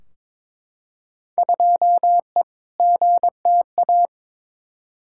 Morse code, heard as a keyed tone, about 22 wpm, 700 Hz. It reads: 2E GTA
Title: Field Day Exchanges Encore - 22wpm